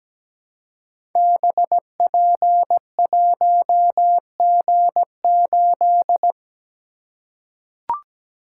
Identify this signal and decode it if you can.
Morse code, heard as a keyed tone, about 17 wpm, 700 Hz. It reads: BP1G8
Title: Code Groups - 5 Letter:Numbers - 17wpm